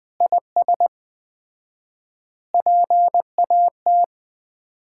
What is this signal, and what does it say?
Morse code, heard as a keyed tone, about 20 wpm, 700 Hz. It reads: IS PAT
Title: Common QSO Phrases - 4x Word-Spacing - 20wpm